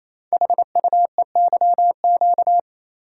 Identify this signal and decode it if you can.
Morse code, heard as a keyed tone, about 27 wpm, 700 Hz. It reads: HUEYQ
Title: Code Groups - 5 Letters - 28wpm